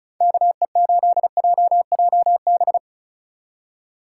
Morse code, heard as a keyed tone, about 34 wpm, 700 Hz. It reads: KE8JJB
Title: USA Call Signs - 6 Characters - 3x Character Spacing - 35wpm